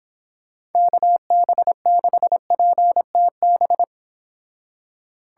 Morse code, heard as a keyed tone, about 29 wpm, 700 Hz. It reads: KB6PTB